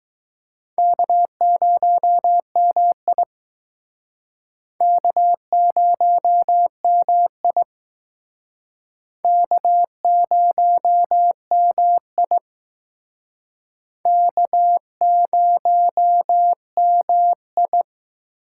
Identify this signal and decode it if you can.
Morse code, heard as a keyed tone, about 20 wpm, 700 Hz. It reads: K0MI K0MI K0MI K0MI